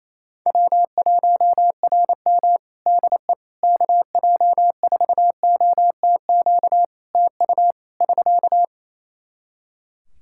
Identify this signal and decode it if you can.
Morse code, heard as a keyed tone, about 28 wpm, 700 Hz. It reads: W1RM DE KJ4OTQ TU <SK>
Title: Practice QSO - Exchange 4 - Ending - 28wpm